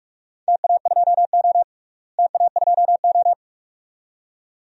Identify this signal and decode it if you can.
Morse code, heard as a keyed tone, about 47 wpm, 700 Hz. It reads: TA2O TA2O
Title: DXCC-100 Call Signs - 4 Characters - Repeated 2x - 45wpm